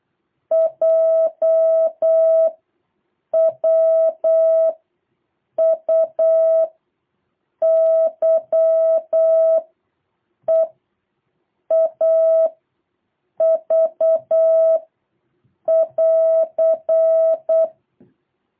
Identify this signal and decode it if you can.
Morse code, heard as a keyed tone, about 8 wpm, 650 Hz. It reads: J W U Y E A V <AR>
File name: jh8cbh / 2023cw